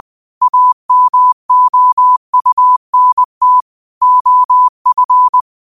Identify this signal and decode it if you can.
Morse code, heard as a keyed tone, about 20 wpm, 1000 Hz. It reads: AMOUNT OF
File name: qotdfiles / 20wpm